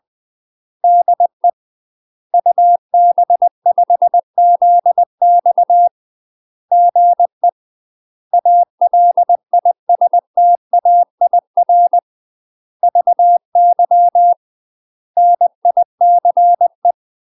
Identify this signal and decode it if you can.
Morse code, heard as a keyed tone, about 20 wpm, 700 Hz. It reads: DE UB5ZX GE ALISTAIR VY NICE
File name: UK MASTER QSOs TRAINING / 20WPM / UK MASTER QSOs 0064